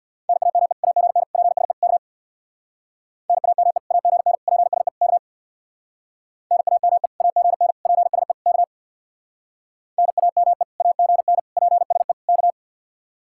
Morse code, heard as a keyed tone, about 64 wpm, 700 Hz. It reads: NAME AGN PSE K NAME AGN PSE K NAME AGN PSE K NAME AGN PSE K